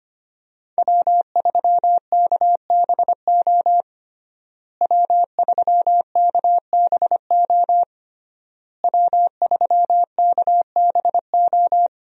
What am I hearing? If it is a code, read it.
Morse code, heard as a keyed tone, about 25 wpm, 700 Hz. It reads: W3KBO W3KBO W3KBO
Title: USA Call Signs - 5 Characters - Repeated 3x - 25wpm